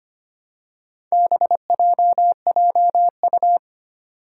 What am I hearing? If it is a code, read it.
Morse code, heard as a keyed tone, about 25 wpm, 700 Hz. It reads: BJJU
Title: Four Challenging Letter-Number Combinations 25wpm